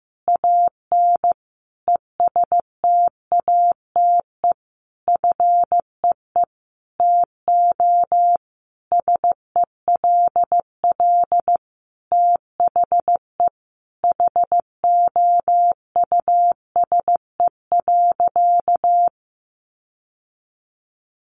Morse code, heard as a keyed tone, about 15 wpm, 700 Hz. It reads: AN ESTATE FEE TO SELL THE HOUSE.
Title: Sentences from Top 1500 Words 15wpm